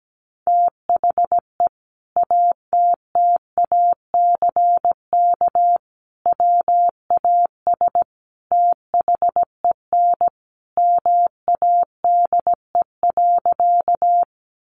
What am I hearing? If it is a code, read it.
Morse code, heard as a keyed tone, about 17 wpm, 700 Hz. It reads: THE ATTACK WAS THEN MADE.
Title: Sentences from Top 1500 Words 17wpm